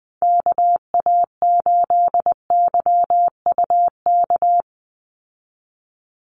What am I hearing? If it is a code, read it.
Morse code, heard as a keyed tone, about 20 wpm, 700 Hz. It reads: KA8YUK
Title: USA Call Signs 20wpm